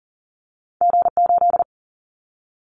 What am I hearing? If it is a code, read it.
Morse code, heard as a keyed tone, about 40 wpm, 700 Hz. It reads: G8
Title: Callsign Prefixes for Top 25 DXCC - 40wpm